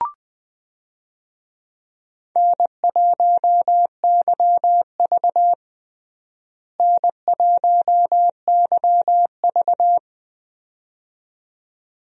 Morse code, heard as a keyed tone, about 20 wpm, 700 Hz. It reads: N1YV N1YV
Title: USA Call Signs - 4 Characters - Repeated 2x - 20wpm